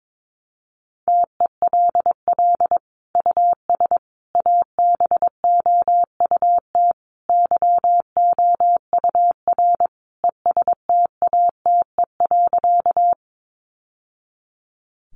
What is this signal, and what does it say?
Morse code, heard as a keyed tone, about 22 wpm, 700 Hz. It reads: TELL US ABOUT YOUR ESTATE.
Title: Sentences from Top 1200 Words 22wpm